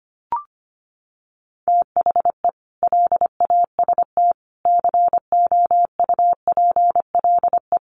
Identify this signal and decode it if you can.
Morse code, heard as a keyed tone, about 25 wpm, 700 Hz. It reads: THE LAST COUPLE
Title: Sets of 5 Words Encore2 25wpm